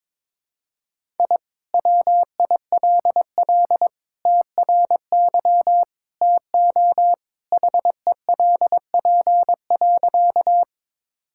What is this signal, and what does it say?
Morse code, heard as a keyed tone, about 22 wpm, 700 Hz. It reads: I WILL TRY TO HELP.
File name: practice / 22wpm / Sentences with 3-Word Phrases - 22wpm